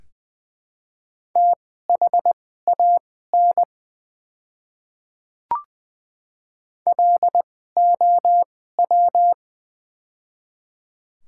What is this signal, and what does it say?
Morse code, heard as a keyed tone, about 20 wpm, spaced as for 14 wpm, 700 Hz. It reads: THAN LOW
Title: 2x Character Spacing - Top 200 Words - 20wpm